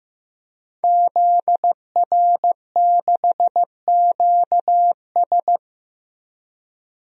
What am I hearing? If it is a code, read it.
Morse code, heard as a keyed tone, about 15 wpm, 700 Hz. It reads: ZR6QS
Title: DXCC-100 Call Signs - 5 Characters - 15wpm